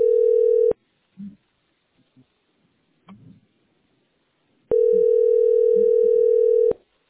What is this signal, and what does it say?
Ringback tone (North America), 440 + 480 Hz.